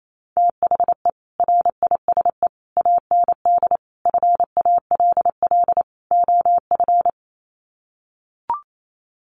Morse code, heard as a keyed tone, about 28 wpm, 700 Hz. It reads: THE RISE AND FALL OF
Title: Sets of 5 Words Encore2 28wpm